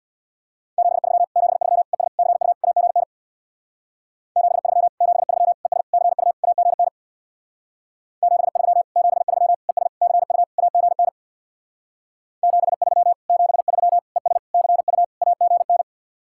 Morse code, heard as a keyed tone, about 64 wpm, 700 Hz. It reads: 73 73 ES CU AGN 73 73 ES CU AGN 73 73 ES CU AGN 73 73 ES CU AGN